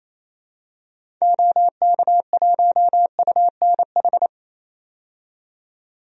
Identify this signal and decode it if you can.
Morse code, heard as a keyed tone, about 28 wpm, 700 Hz. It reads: OK1UNH
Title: DXCC-100 Call Signs - 6 Characters - 28wpm